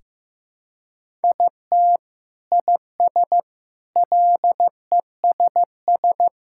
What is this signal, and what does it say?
Morse code, heard as a keyed tone, about 16 wpm, 700 Hz. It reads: IT IS LESS
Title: Sentences from Top 1000 Words - Review Part 6 - 15wpm